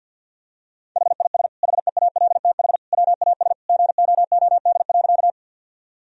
Morse code, heard as a keyed tone, about 50 wpm, 700 Hz. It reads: HIS HEALTH WAS GOOD.
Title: Sentences from Top 1000 Words - Review Part 3 - 50wpm